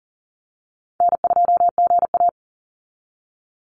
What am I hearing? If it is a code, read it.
Morse code, heard as a keyed tone, about 40 wpm, 700 Hz. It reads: N2GA
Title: Easy Call Signs - 40wpm